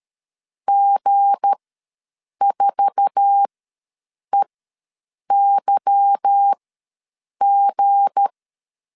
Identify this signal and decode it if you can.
Morse code, heard as a keyed tone, about 13 wpm, spaced as for 7 wpm, 800 Hz. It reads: G4EYG
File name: LEZIONI / 8 YJ45 / 8 YJ4 ws 12wpm 3249